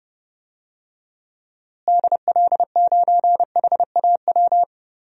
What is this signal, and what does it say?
Morse code, heard as a keyed tone, about 30 wpm, 700 Hz. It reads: DL9HAW
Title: DXCC-100 Call Signs - 6 Characters - Repeated 3x - 30wpm